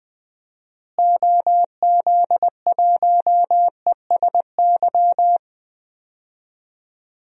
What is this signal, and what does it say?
Morse code, heard as a keyed tone, about 20 wpm, 700 Hz. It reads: OZ1ESY
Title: DXCC-100 Call Signs - 6 Characters - Repeated 2x - 20wpm